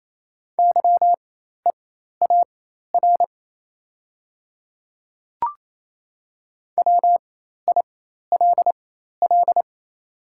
Morse code, heard as a keyed tone, about 28 wpm, spaced as for 13 wpm, 700 Hz. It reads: YEAR WILL
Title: 4x Character Spacing - Top 500 Words - 28wpm